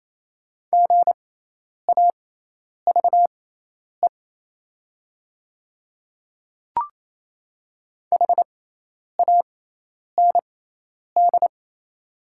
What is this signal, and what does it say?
Morse code, heard as a keyed tone, about 28 wpm, spaced as for 10 wpm, 690 Hz. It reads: GAVE HAND
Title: 6x Character Spacing - Top 500 Words - 28wpm